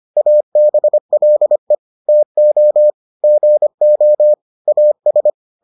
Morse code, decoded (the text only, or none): ABLE TO GO AS